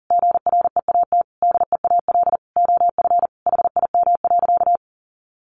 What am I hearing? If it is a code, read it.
Morse code, heard as a keyed tone, about 40 wpm, 700 Hz. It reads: GREAT DEAL OF HIM.
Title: Sentences from Top 800 Words 40wpm